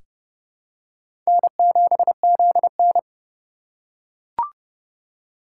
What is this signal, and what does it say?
Morse code, heard as a keyed tone, about 30 wpm, 700 Hz. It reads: N7ZN
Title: K1USN SST QSO Practice - 30wpm